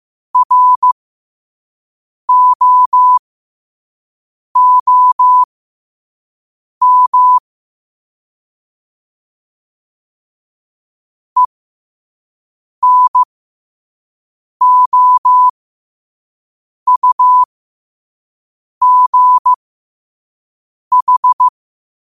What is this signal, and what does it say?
Morse code, heard as a keyed tone, about 15 wpm, spaced as for 5 wpm, 1000 Hz. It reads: ROOM ENOUGH
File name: qotdfiles / 5wpm